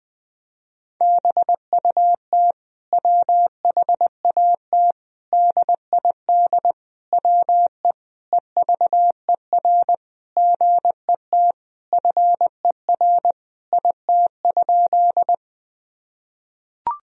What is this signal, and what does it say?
Morse code, heard as a keyed tone, about 19 wpm, 700 Hz. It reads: BUT WHAT DID WE EVER GET FER IT?
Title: Sentences from Top 400 Words and Abbreviations 20wpm